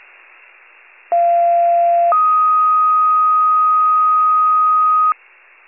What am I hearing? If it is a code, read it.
Two-tone page, heard as a one-second tone, then a three-second tone, about 690 then 1250 Hz. It is heard from 1.1 to 5.1 s.
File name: wcalert